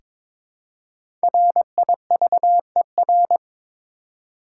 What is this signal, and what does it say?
Morse code, heard as a keyed tone, about 23 wpm, 700 Hz. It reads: RIVER
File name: 5 Letter Words - First 3 Letters Given - 22wpm